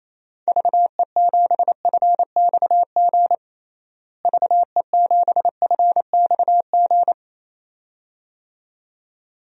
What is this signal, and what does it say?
Morse code, heard as a keyed tone, about 28 wpm, 700 Hz. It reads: VE7FXG VE7FXG
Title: DXCC-100 Call Signs - 6 Characters - Repeated 2x - 28wpm